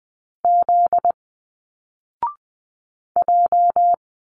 Morse code, heard as a keyed tone, about 20 wpm, 700 Hz. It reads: Z J